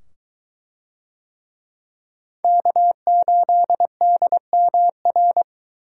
Morse code, heard as a keyed tone, about 21 wpm, 700 Hz. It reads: K8DMR